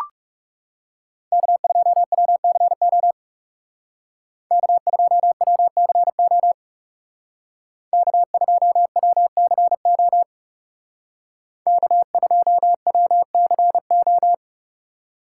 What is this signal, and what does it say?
Morse code, heard as a keyed tone, about 37 wpm, 700 Hz. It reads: K2WCO K2WCO K2WCO K2WCO
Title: Speed Racing - USA 1x3 Call Signs 30wpm